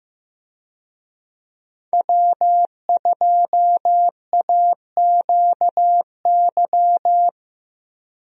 Morse code, heard as a keyed tone, about 15 wpm, 700 Hz. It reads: W2AQY
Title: USA Call Signs - 5 Characters - Repeated 3x - 15wpm